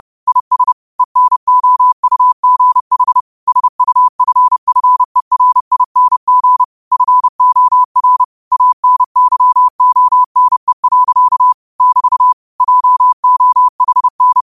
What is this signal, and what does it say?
Morse code, heard as a keyed tone, about 30 wpm, 1000 Hz. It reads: IS ENOUGH SUFFERING FOR ANYONE. <BT> JOHN
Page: New quotes every day in morse code at 30 Words per minute.